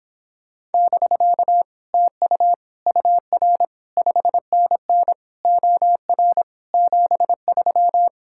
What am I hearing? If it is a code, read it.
Morse code, heard as a keyed tone, about 31 wpm, 700 Hz. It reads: <BK> TU UR 5NN OR 73